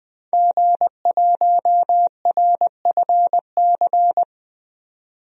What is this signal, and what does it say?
Morse code, heard as a keyed tone, about 20 wpm, 700 Hz. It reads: G1RFC
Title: DXCC-100 Call Signs - 5 Characters - 20wpm